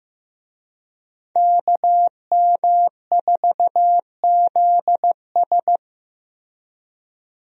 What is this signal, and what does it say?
Morse code, heard as a keyed tone, about 16 wpm, 700 Hz. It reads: KM4ZS
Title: USA Call Signs - 5 Characters - Repeated 2x - 15wpm